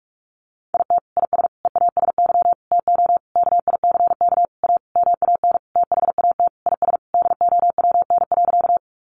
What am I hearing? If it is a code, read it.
Morse code, heard as a keyed tone, about 44 wpm, 700 Hz. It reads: IT IS EASY TO KICK A MAN THAT IS DOWN.
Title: Sentences from Top 1300 Words 45wpm